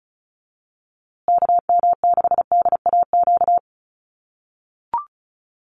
Morse code, heard as a keyed tone, about 34 wpm, 700 Hz. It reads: KM6DAQ
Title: USA Call Signs Repeated 3x 35wpm